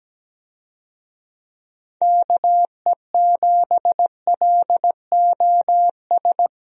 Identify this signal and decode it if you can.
Morse code, heard as a keyed tone, about 17 wpm, 700 Hz. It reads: KE7LOS
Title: USA Call Signs - 6 Characters - Repeated 3x - 17wpm